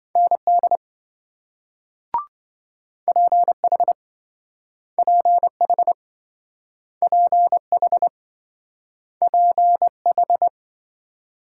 Morse code, heard as a keyed tone, about 27 wpm, 700 Hz. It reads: ND PH PH PH PH